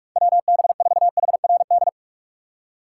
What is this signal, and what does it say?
Morse code, heard as a keyed tone, about 46 wpm, 700 Hz. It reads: WB4HRD
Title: USA Call Signs - 6 Characters - 3x Character Spacing - 45wpm